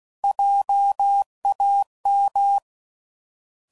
Morse code, heard as a keyed tone, about 16 wpm, 800 Hz. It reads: JAM